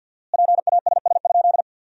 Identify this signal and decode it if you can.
Morse code, heard as a keyed tone, about 50 wpm, 700 Hz. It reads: PASS?
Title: 4x Word-Spacing - Sentences from Top 400 Words - 50wpm